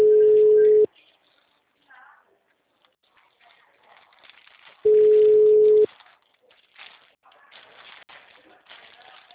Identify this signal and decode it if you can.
Ringback tone (Europe, 425 Hz), with speech.